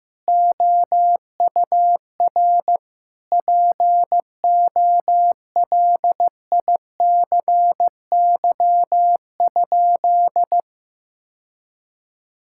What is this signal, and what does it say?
Morse code, heard as a keyed tone, about 15 wpm, 700 Hz. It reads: OUR POLICY?
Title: Sentences from Top 1000 Words - Review Part 5 - 15wpm